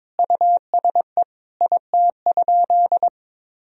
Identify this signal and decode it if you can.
Morse code, heard as a keyed tone, about 22 wpm, 700 Hz. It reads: USE IT?
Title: Sentences from Top 200 Words and Abbreviations 22wpm